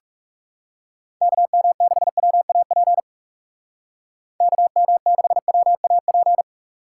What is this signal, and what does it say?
Morse code, heard as a keyed tone, about 36 wpm, 700 Hz. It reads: KM6WAP KM6WAP
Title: Speed Racing - USA Call Signs 30wpm